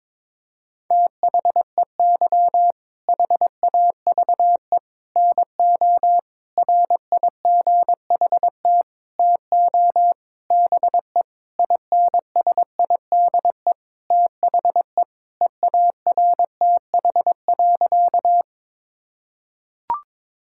Morse code, heard as a keyed tone, about 21 wpm, 700 Hz. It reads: THEY HAVE NO RIGHT TO BE INSIDE THE EARTH.
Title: Sentences from Top 1000 Words - Review Part 6 - 22wpm